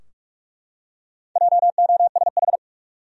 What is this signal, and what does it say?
Morse code, heard as a keyed tone, about 41 wpm, 700 Hz. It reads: JOSH
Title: USA Common Names - Max 5 Letters - 45wpm